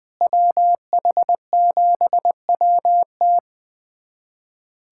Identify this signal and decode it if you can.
Morse code, heard as a keyed tone, about 21 wpm, 700 Hz. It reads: WH7WT
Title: USA Callsign DE Callsign - 5 Characters - 20wpm